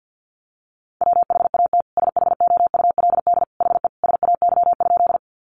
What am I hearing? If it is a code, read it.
Morse code, heard as a keyed tone, about 50 wpm, 700 Hz. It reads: WHAT SHOULD HE SAY?
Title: Sentences from Top 300 Words 50wpm